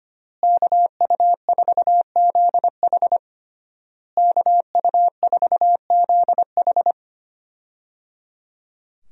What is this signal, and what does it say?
Morse code, heard as a keyed tone, about 25 wpm, 700 Hz. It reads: KU4ZH KU4ZH
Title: USA Call Signs - 5 Characters - Repeated 3x - 25wpm